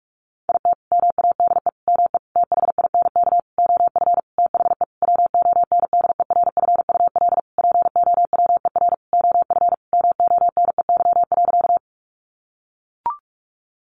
Morse code, heard as a keyed tone, about 44 wpm, 700 Hz. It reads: IT MADE ME THINK OF THE WONDERFUL POWER OF MONEY.
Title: Sentences from Top 1100 Words 45wpm